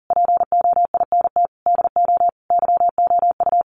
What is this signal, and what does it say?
Morse code, heard as a keyed tone, about 40 wpm, 700 Hz. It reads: POINT DO YOU
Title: Sentences from Top 1300 Words 40wpm